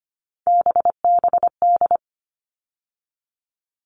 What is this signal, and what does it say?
Morse code, heard as a keyed tone, about 25 wpm, 700 Hz. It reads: BBD
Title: All Cut Numbers - 3-Digit 25wpm